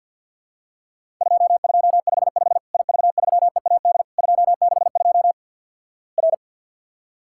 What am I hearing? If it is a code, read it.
Morse code, heard as a keyed tone, about 50 wpm, 700 Hz. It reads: 2255 IV3EAD 262 R
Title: Worked All Europe DX Contest - QTC Exchange - 50wpm